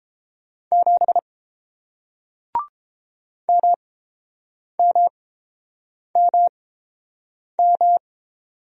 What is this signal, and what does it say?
Morse code, heard as a keyed tone, about 30 wpm, spaced as for 7 wpm, 700 Hz. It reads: 7 MMMM